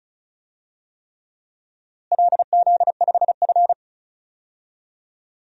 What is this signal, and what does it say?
Morse code, heard as a keyed tone, about 36 wpm, 700 Hz. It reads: LZ5F
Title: DXCC-100 Call Signs - 4 Characters - Repeated 3x - 35wpm